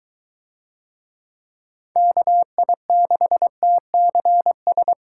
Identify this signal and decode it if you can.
Morse code, heard as a keyed tone, about 22 wpm, 700 Hz. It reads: KI6TCS